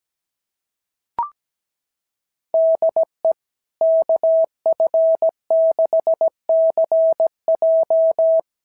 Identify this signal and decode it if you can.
Morse code, heard as a keyed tone, about 17 wpm, 690 Hz. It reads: DE KF6CJ
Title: SOTA QSO Practice 17wpm